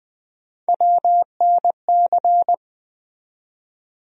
Morse code, heard as a keyed tone, about 20 wpm, 700 Hz. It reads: WNC
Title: USA Call Sign Postfixes - 3 Letters 20wpm